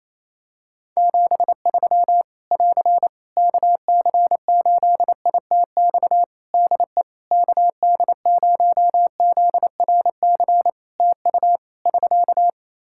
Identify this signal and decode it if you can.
Morse code, heard as a keyed tone, about 28 wpm, 700 Hz. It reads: 73 <AR> KC8ITX DE KD0ZRC TU <SK>
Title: Practice QSO - Exchange 4 - Ending - 28wpm